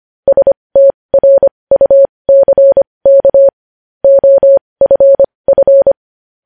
Morse code, heard as a keyed tone, about 25 wpm, spaced as for 20 wpm, 550 Hz. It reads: STRUCK OFF
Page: news-headlines-in-morse-code-at-20-wpm